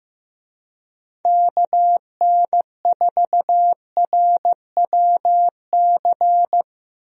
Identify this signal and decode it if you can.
Morse code, heard as a keyed tone, about 15 wpm, 700 Hz. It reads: KN4RWC
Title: USA Call Signs - 6 Characters - Repeated 2x - 15wpm